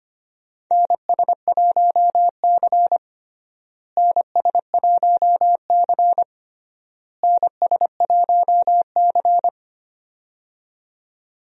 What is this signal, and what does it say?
Morse code, heard as a keyed tone, about 25 wpm, 700 Hz. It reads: NS1C NS1C NS1C
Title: USA Call Signs - 4 Characters - Repeated 3x - 25wpm